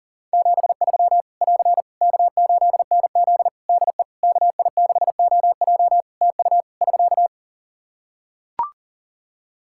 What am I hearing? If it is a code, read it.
Morse code, heard as a keyed tone, about 40 wpm, 700 Hz. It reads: 73 <AR> K8NZ DE KI6OJ TU <SK>
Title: Practice QSO - Exchange 4 - Ending - 40wpm